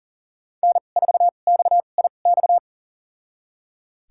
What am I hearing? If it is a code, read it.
Morse code, heard as a keyed tone, about 37 wpm, spaced as for 28 wpm, 700 Hz. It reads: N4XIX
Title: USA Call Signs - 2x Character Spacing - 40wpm